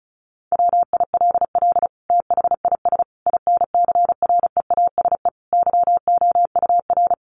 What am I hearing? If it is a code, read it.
Morse code, heard as a keyed tone, about 34 wpm, 700 Hz. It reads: WILL THIS INCREASE YOUR
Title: Sentences from Top 1200 Words 35wpm